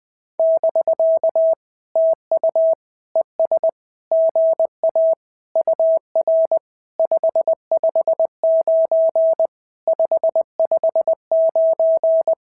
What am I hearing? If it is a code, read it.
Morse code, heard as a keyed tone, about 17 wpm, 650 Hz. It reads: <BK> TU ES GA UR 559 559
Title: SOTA QSO Practice 20wpm